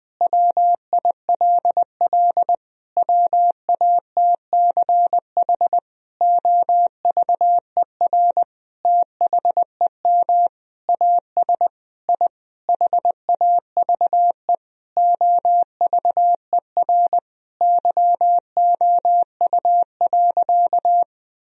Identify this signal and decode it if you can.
Morse code, heard as a keyed tone, about 20 wpm, 700 Hz. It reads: WILL WATCH OVER THEM AS I HAVE OVER YOU.
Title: Sentences from Top 1000 Words - Review Part 1 - 20wpm